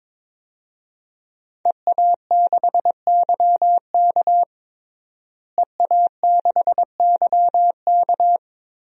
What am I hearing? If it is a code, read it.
Morse code, heard as a keyed tone, about 21 wpm, 700 Hz. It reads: EA6YK EA6YK
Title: DXCC-100 Call Signs - 5 Characters - Repeated 3x - 22wpm